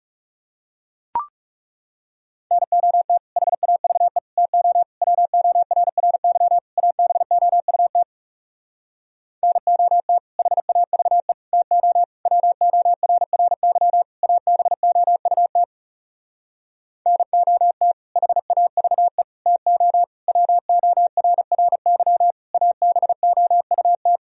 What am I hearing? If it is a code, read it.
Morse code, heard as a keyed tone, about 40 wpm, 700 Hz. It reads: NOT HAVE TO WORRY ABOUT NOT HAVE TO WORRY ABOUT NOT HAVE TO WORRY ABOUT